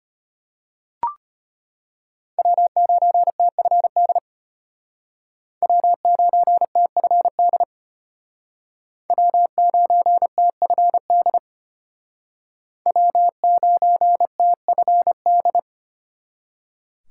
Morse code, heard as a keyed tone, about 33 wpm, 700 Hz. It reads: W9TFD W9TFD W9TFD W9TFD